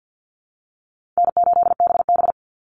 Morse code, heard as a keyed tone, about 50 wpm, 700 Hz. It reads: N8BB
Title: NA SOTA Chaser Callsigns 50wpm